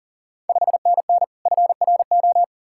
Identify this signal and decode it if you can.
Morse code, heard as a keyed tone, about 40 wpm, 700 Hz. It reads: 5NN FRO
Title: Ontario QSO Party Exchange 40wpm